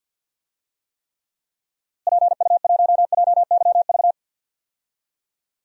Morse code, heard as a keyed tone, about 53 wpm, 700 Hz. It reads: PU1JYV
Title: DXCC-100 Call Signs - 6 Characters - Repeated 2x - 50wpm